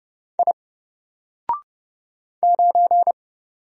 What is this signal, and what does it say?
Morse code, heard as a keyed tone, about 30 wpm, 700 Hz. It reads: I 9
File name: Single-Letter-Number-30wpm